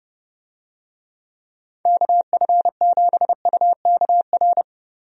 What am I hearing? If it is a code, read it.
Morse code, heard as a keyed tone, about 32 wpm, 700 Hz. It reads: KF7UKR
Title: USA Call Signs - 6 Characters - Repeated 2x - 30wpm